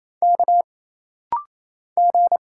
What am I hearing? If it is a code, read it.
Morse code, heard as a keyed tone, about 28 wpm, 700 Hz. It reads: K G
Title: Single Letters - Rapid-Fire 28wpm